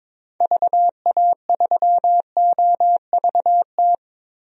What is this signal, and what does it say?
Morse code, heard as a keyed tone, about 21 wpm, 700 Hz. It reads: VA3OVT
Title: DXCC-100 Call Signs - 6 Characters - Repeated 3x - 22wpm